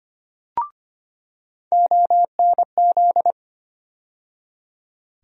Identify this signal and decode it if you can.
Morse code, heard as a keyed tone, about 25 wpm, 700 Hz. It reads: ONZ
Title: USA Call Sign Postfixes - 3 Letters 25wpm